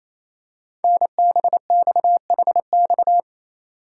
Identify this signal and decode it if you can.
Morse code, heard as a keyed tone, about 29 wpm, 700 Hz. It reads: NBXHX
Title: Code Groups - 5 Letters - 28wpm